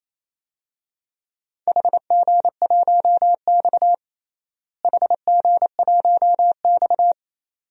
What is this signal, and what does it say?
Morse code, heard as a keyed tone, about 28 wpm, 700 Hz. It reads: HG1X HG1X
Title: DXCC-100 Call Signs - 4 Characters - Repeated 2x - 28wpm